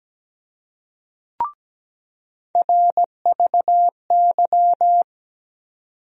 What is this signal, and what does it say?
Morse code, heard as a keyed tone, about 17 wpm, 700 Hz. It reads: RVY
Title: USA Call Sign Postfixes - 3 Letters 17wpm